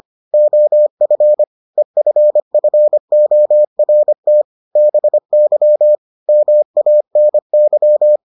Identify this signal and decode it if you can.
Morse code, heard as a keyed tone, about 25 wpm, 600 Hz. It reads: OF EFFORT BY MANY